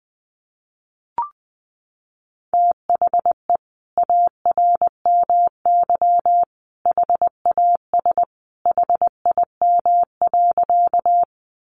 Morse code, heard as a keyed tone, about 20 wpm, 700 Hz. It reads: THE ARMY HAS HIM.
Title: Sentences from Top 800 Words 20wpm